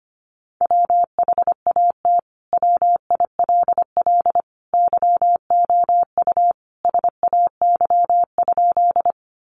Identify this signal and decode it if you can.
Morse code, heard as a keyed tone, about 25 wpm, 700 Hz. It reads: WHAT WILL YOU SAY?
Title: Sentences from Top 200 Words 25wpm